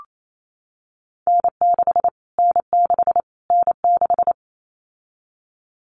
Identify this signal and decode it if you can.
Morse code, heard as a keyed tone, about 28 wpm, 700 Hz. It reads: N6 N6 N6
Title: Callsign Prefixes for Top 25 DXCC - Repeated 3x - 28wpm